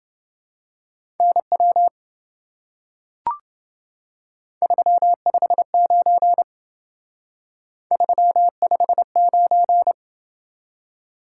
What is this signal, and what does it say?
Morse code, heard as a keyed tone, about 28 wpm, 700 Hz. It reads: NW 359 359